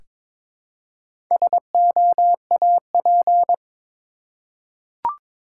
Morse code, heard as a keyed tone, about 22 wpm, 690 Hz. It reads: SOAP
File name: Difficult Words with Hints - 22wpm